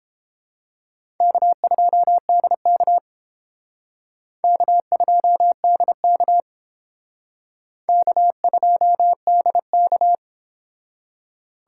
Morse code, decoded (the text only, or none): K2DK K2DK K2DK